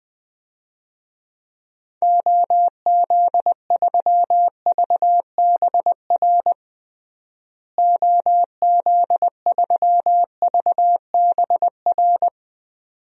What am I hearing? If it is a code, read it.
Morse code, heard as a keyed tone, about 21 wpm, 700 Hz. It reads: OZ3VBR OZ3VBR
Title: DXCC-100 Call Signs - 6 Characters - Repeated 2x - 20wpm